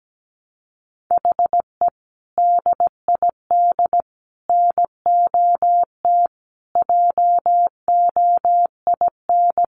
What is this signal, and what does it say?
Morse code, heard as a keyed tone, about 17 wpm, 700 Hz. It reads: HE DID NOT JOIN
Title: Sentences from Top 1400 Words 17wpm